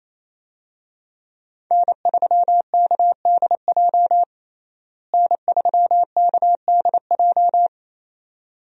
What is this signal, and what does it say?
Morse code, heard as a keyed tone, about 28 wpm, 700 Hz. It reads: N3KDJ N3KDJ
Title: USA Call Signs - 5 Characters - Repeated 2x - 28wpm